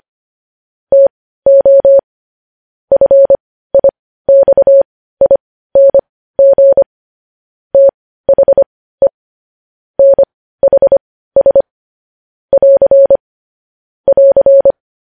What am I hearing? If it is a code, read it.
Morse code, heard as a keyed tone, about 25 wpm, spaced as for 15 wpm, 550 Hz. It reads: TO FIXING THE NHS <AR> <AR>
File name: podcast / News headlines-1x-25x15